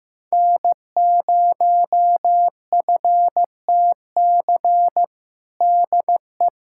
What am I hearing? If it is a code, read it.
Morse code, heard as a keyed tone, about 15 wpm, 700 Hz. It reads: N0FTC DE
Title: USA Callsign DE Callsign - 5 Characters - 15wpm